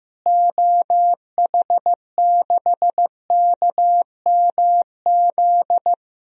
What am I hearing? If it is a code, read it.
Morse code, heard as a keyed tone, about 15 wpm, 700 Hz. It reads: OH6KMZ
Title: DXCC-100 Call Signs - 6 Characters - Repeated 3x - 15wpm